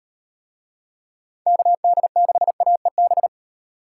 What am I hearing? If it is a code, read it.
Morse code, heard as a keyed tone, about 38 wpm, 700 Hz. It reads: KD6AEB